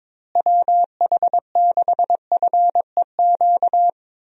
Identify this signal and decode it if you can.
Morse code, heard as a keyed tone, about 23 wpm, 700 Hz. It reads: WH6FEQ
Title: USA Call Signs - 6 Characters - Repeated 2x - 22wpm